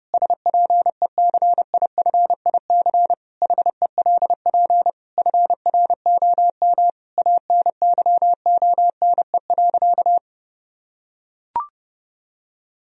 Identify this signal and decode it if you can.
Morse code, heard as a keyed tone, about 30 wpm, 700 Hz. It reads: SPECIFIC HELP FROM ANYONE.
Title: Sentences from Top 1000 Words - Review Part 3 - 30wpm